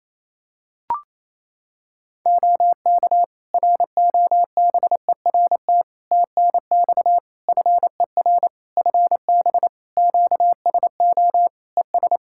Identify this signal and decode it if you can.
Morse code, heard as a keyed tone, about 28 wpm, 700 Hz. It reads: OK ROBERT TNX FER FB QSO ES
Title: Practice QSO - Exchange 4 - Ending - 28wpm